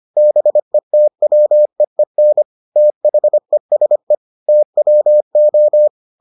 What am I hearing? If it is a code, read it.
Morse code, heard as a keyed tone, about 25 wpm, 600 Hz. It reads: BETWEEN THESE TWO